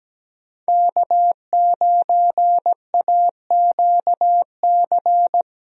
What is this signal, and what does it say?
Morse code, heard as a keyed tone, about 17 wpm, 700 Hz. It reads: K9AQC
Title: USA Call Signs - 5 Characters - Repeated 2x - 17wpm